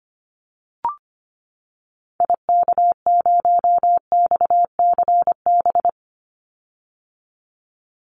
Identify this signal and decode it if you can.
Morse code, heard as a keyed tone, about 25 wpm, 700 Hz. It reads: IK0XCB
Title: Call Signs - 25wpm